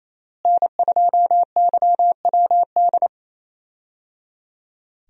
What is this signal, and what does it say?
Morse code, heard as a keyed tone, about 28 wpm, 700 Hz. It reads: N2YWD
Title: USA Call Signs - 5 Characters - Repeated 3x - 28wpm